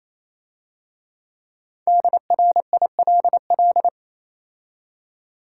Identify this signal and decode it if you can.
Morse code, heard as a keyed tone, about 27 wpm, 690 Hz. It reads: DRILL
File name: Easy Words - Speed-Racing with Hints - 22wpm